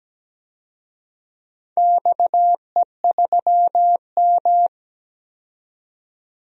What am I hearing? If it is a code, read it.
Morse code, heard as a keyed tone, about 17 wpm, 700 Hz. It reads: XE3M
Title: DXCC-100 Call Signs - 4 Characters - 17wpm